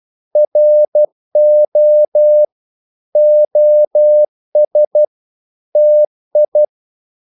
Morse code, heard as a keyed tone, about 12 wpm, 600 Hz. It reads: RO OS TI
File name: Korte ord DK 12wpm